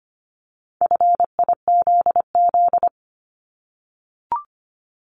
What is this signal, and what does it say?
Morse code, heard as a keyed tone, about 25 wpm, 700 Hz. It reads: FIZZ
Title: Multi-letter Phonemes 25wpm